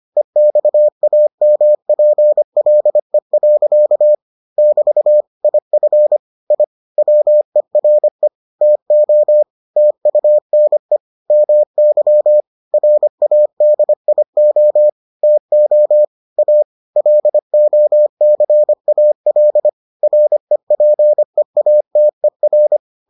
Morse code, heard as a keyed tone, about 25 wpm, 600 Hz. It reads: EXAMPLE. <BT> IF I WERE TO TUNE MY RADIO TO A LOCAL REPEATER